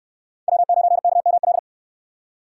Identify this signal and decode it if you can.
Morse code, heard as a keyed tone, about 60 wpm, 700 Hz. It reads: W9KML